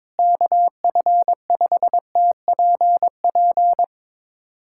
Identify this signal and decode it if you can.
Morse code, heard as a keyed tone, about 21 wpm, 700 Hz. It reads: KF5TPP
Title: USA Callsign DE Callsign - 6 Characters - 22wpm